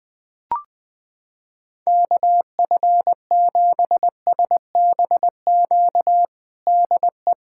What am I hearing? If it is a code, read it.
Morse code, heard as a keyed tone, about 20 wpm, 700 Hz. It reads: KF7SBQ DE
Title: USA Callsign DE Callsign 20wpm